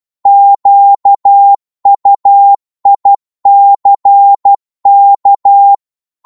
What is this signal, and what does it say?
Morse code, heard as a keyed tone, about 12 wpm, 800 Hz. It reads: QUICK